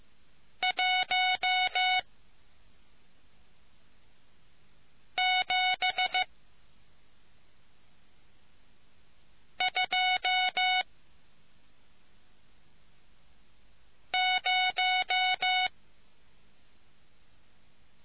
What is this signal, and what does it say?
Morse code, heard as a keyed tone, about 15 wpm, 740 Hz. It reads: 1 7 2 0